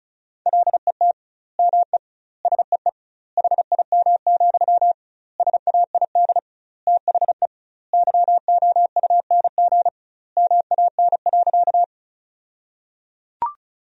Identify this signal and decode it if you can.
Morse code, heard as a keyed tone, about 36 wpm, 700 Hz. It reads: LET ME SEE HIM, SAID THE YOUNG MAN.
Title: 2x Word-Spacing - Sentences from Top 400 Words - 35wpm